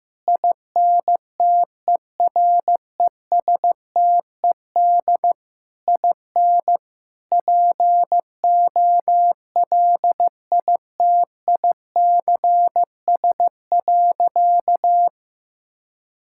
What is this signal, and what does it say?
Morse code, heard as a keyed tone, about 15 wpm, 700 Hz. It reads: INTERESTED IN POLITICS.
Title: Sentences from Top 1000 Words - Review Part 6 - 15wpm